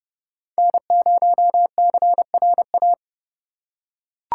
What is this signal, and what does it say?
Morse code, heard as a keyed tone, about 32 wpm, 700 Hz. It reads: N0CRA
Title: USA Call Signs - 5 Characters - 8x Character Spacing - 30wpm